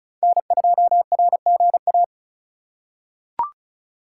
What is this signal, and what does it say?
Morse code, heard as a keyed tone, about 36 wpm, 700 Hz. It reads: N2RGA
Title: USA Call Signs - 5 Characters - 8x Character Spacing - 35wpm